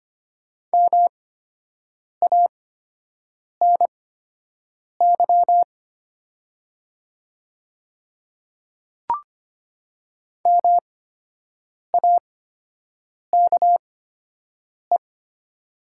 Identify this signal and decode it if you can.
Morse code, heard as a keyed tone, about 25 wpm, spaced as for 7 wpm, 700 Hz. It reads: MANY MAKE
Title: 8x Character Spacing - Top 300 Words - 25wpm